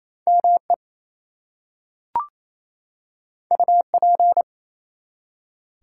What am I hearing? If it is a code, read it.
Morse code, heard as a keyed tone, about 27 wpm, 700 Hz. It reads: ME UP
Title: Two Letter Words - 28wpm